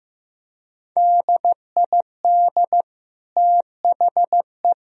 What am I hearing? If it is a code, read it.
Morse code, heard as a keyed tone, about 15 wpm, 700 Hz. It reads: DID THE
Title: Sentences from Top 1000 Words - Review Part 4 - 15wpm